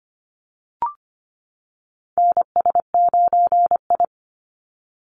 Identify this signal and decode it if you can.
Morse code, heard as a keyed tone, about 25 wpm, 700 Hz. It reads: NS9I
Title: Call Signs - 25wpm